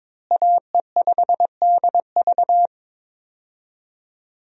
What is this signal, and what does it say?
Morse code, heard as a keyed tone, about 23 wpm, 700 Hz. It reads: AE5DV
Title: USA Call Signs - 5 Characters - Repeated 2x - 22wpm